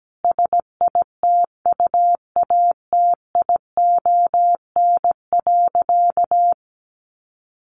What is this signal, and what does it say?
Morse code, heard as a keyed tone, about 17 wpm, 700 Hz. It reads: SITUATION.
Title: Sentences from Top 1200 Words 17wpm